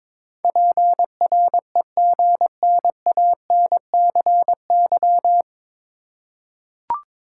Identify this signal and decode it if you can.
Morse code, heard as a keyed tone, about 22 wpm, 700 Hz. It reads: PREGNANCY
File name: 9 Letter Words - First 3 Letters Given - 22wpm